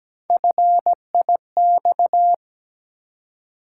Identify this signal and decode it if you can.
Morse code, heard as a keyed tone, about 17 wpm, 700 Hz. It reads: FIX
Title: Three Letter Words - 17wpm